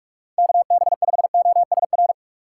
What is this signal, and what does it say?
Morse code, heard as a keyed tone, about 41 wpm, 700 Hz. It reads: KB5OSR